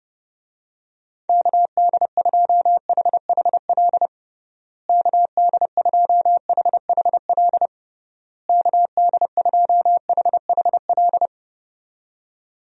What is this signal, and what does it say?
Morse code, heard as a keyed tone, about 30 wpm, 700 Hz. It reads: KD2HHL KD2HHL KD2HHL
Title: USA Call Signs - 6 Characters - Repeated 3x - 30wpm